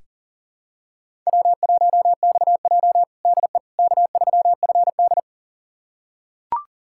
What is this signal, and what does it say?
Morse code, heard as a keyed tone, about 40 wpm, 700 Hz. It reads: W1XJ DE K3FD
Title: USA Callsign DE Callsign - 4 Characters - 40wpm